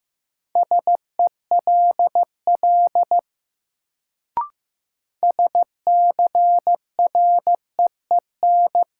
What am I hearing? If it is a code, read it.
Morse code, heard as a keyed tone, about 15 wpm, 700 Hz. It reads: SELL SCREEN
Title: Sentences from Top 1000 Words - Review Part 6 - New Word Drill - 15wpm